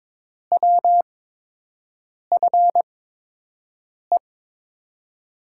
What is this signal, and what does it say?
Morse code, heard as a keyed tone, about 22 wpm, spaced as for 6 wpm, 700 Hz. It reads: WFE
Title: USA Call Signs - 6 Characters - 8x Character Spacing - 22wpm